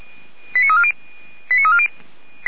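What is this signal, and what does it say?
Five-tone selective call (ZVEI), heard as quick runs of five tones, tones of about 70 ms. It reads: 89340 89340
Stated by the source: ZVEI five-tone selective call received by SDR, from the Feuerwehr Orlamünde fire brigade audio archive.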